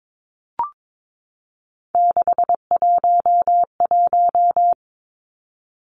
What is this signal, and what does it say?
Morse code, heard as a keyed tone, about 22 wpm, 700 Hz. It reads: 611
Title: Numbers - 3-Digit 22wpm